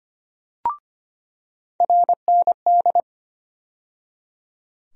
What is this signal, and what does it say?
Morse code, heard as a keyed tone, about 25 wpm, 700 Hz. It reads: RND
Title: USA Call Sign Postfixes - 3 Letters 25wpm